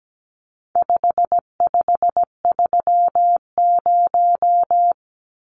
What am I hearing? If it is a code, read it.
Morse code, heard as a keyed tone, about 17 wpm, 700 Hz. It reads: 5530
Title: Numbers - 4-Digit 17wpm